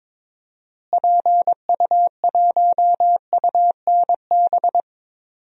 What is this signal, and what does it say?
Morse code, heard as a keyed tone, about 21 wpm, 700 Hz. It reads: PU1UNB
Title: DXCC-100 Call Signs - 6 Characters - Repeated 2x - 22wpm